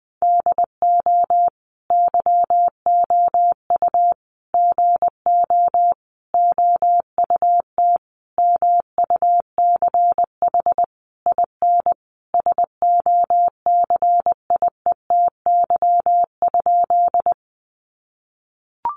Morse code, heard as a keyed tone, about 20 wpm, 700 Hz. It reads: DO YOU GO OUT MUCH IN SOCIETY?
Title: Sentences from Top 700 Words 20wpm